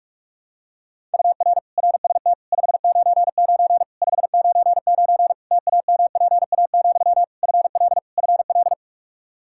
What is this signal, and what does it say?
Morse code, heard as a keyed tone, about 41 wpm, 700 Hz. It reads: UR RST 599 599 TAMPA, FL FL
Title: K1USN SST QSO Practice 45wpm